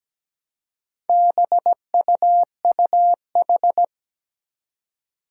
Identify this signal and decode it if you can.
Morse code, heard as a keyed tone, about 17 wpm, 700 Hz. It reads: BUUH
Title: Four Challenging Letter-Number Combinations 17wpm